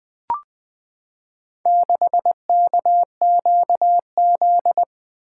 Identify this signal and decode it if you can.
Morse code, heard as a keyed tone, about 20 wpm, 700 Hz. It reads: 6KQZ
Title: Four Letter-Number Combinations - 20wpm